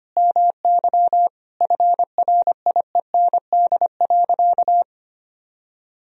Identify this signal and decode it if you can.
Morse code, heard as a keyed tone, about 25 wpm, 700 Hz. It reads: MY FRIEND.
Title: Sentences from Top 300 Words and Abbreviations 25wpm